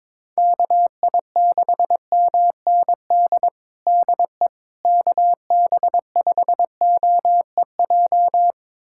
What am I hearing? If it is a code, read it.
Morse code, heard as a keyed tone, about 22 wpm, 700 Hz. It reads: KI6MND DE KB5OEJ
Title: USA Callsign DE Callsign 22wpm